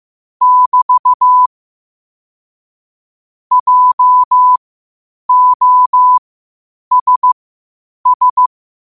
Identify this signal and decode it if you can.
Morse code, heard as a keyed tone, about 15 wpm, spaced as for 8 wpm, 1000 Hz. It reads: <BT> JOSS
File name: qotdfiles / 8wpm